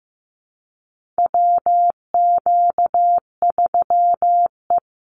Morse code, heard as a keyed tone, about 15 wpm, 700 Hz. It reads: WQ3E
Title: Call Signs Repeated 3x - 15wpm